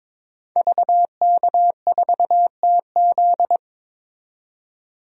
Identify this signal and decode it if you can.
Morse code, heard as a keyed tone, about 22 wpm, 700 Hz. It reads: VK4TZ
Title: DXCC-100 Call Signs - 5 Characters - 22wpm